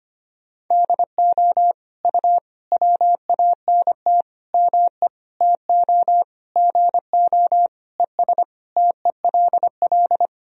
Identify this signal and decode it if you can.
Morse code, heard as a keyed tone, about 25 wpm, 700 Hz. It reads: DO U WANT ME TO GO ES TELL
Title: Sentences from Top 200 Words and Abbreviations 25wpm